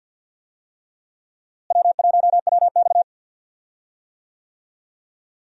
Morse code, heard as a keyed tone, about 47 wpm, 700 Hz. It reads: W1WX
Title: USA Call Signs - 4 Characters - Repeated 3x - 50wpm